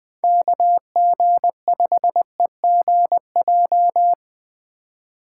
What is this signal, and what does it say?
Morse code, heard as a keyed tone, about 20 wpm, 700 Hz. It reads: KG5EGJ
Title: USA Call Signs - 6 Characters - 2x Character Spacing - 20wpm